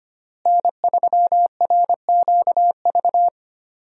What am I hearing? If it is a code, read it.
Morse code, heard as a keyed tone, about 25 wpm, 700 Hz. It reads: N3RQV
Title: USA Call Signs - 5 Characters - Repeated 3x - 25wpm